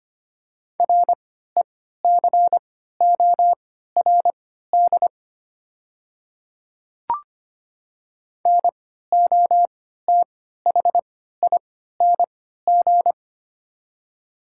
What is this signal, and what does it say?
Morse code, heard as a keyed tone, about 25 wpm, spaced as for 14 wpm, 700 Hz. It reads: RECORD NOTHING
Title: 3x Character Spacing - Top 500 Words - 25wpm